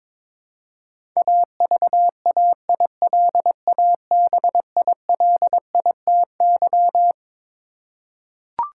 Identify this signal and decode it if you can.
Morse code, heard as a keyed tone, about 22 wpm, 700 Hz. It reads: AVAILABILITY
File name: Twelve Letter Words - 22wpm